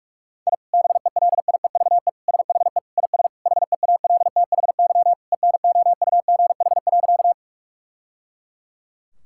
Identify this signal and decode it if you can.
Morse code, heard as a keyed tone, about 46 wpm, 700 Hz. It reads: I BELIEVE SHE IS HEALTHY ENOUGH.
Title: Sentences from Top 1000 Words - Review Part 5 - 45wpm